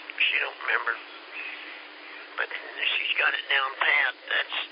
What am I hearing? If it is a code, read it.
Emergency Alert System (SAME) tones, heard as harsh buzzing bursts, with speech. It is heard from 0.6 to 4.2 s.